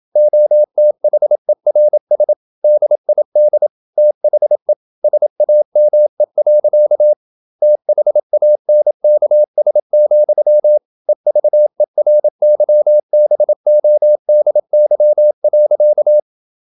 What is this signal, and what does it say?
Morse code, heard as a keyed tone, about 27 wpm, 600 Hz. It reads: OTHERS DID THE SAME. THANKS, EVERYBODY.
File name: Never 0027wpm